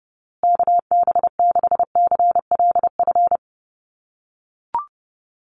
Morse code, heard as a keyed tone, about 30 wpm, 700 Hz. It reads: KB6CLF
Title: USA Call Signs Repeated 3x 30wpm